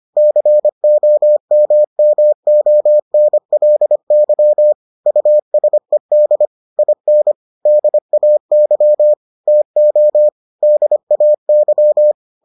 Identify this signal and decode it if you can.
Morse code, heard as a keyed tone, about 25 wpm, 600 Hz. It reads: COMMONLY USED IN DAY TO DAY